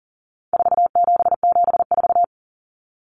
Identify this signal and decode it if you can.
Morse code, heard as a keyed tone, about 40 wpm, 700 Hz. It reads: 4774
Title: Common Cut Numbers - 4-Digit 40wpm